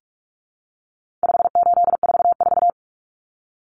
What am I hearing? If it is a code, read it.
Morse code, heard as a keyed tone, about 44 wpm, 700 Hz. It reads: 5844
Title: Common Cut Numbers - 4-Digit 45wpm